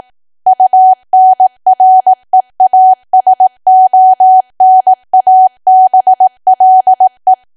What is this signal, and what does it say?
Morse code, heard as a keyed tone, about 18 wpm, 750 Hz. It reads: UNREASONABLE
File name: cwa / W1AW-Level2 / 18WPM